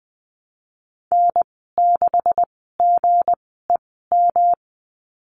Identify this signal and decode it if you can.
Morse code, heard as a keyed tone, about 20 wpm, spaced as for 14 wpm, 700 Hz. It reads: N6GEM
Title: 161 Lesson 26g - Letter g - Call Signs 20wpm